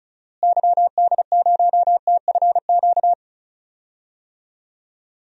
Morse code, heard as a keyed tone, about 36 wpm, 700 Hz. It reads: YD0TFQ
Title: DXCC-100 Call Signs - 6 Characters - Repeated 3x - 35wpm